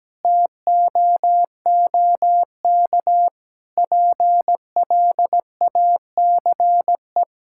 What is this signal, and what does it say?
Morse code, heard as a keyed tone, about 17 wpm, 700 Hz. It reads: TOOK PLACE
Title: Sentences from Top 1000 Words - Review Part 6 - 17wpm